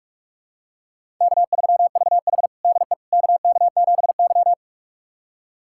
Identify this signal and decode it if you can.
Morse code, heard as a keyed tone, about 46 wpm, 700 Hz. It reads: K3VH DE KK7Y
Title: USA Callsign DE Callsign - 4 Characters - 45wpm